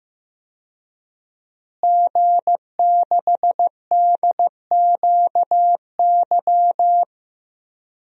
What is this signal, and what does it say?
Morse code, heard as a keyed tone, about 15 wpm, 700 Hz. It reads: G6DQY
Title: DXCC-100 Call Signs - 5 Characters - Repeated 2x - 15wpm